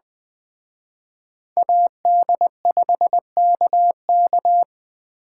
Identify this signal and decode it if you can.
Morse code, heard as a keyed tone, about 21 wpm, 700 Hz. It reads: AD5KK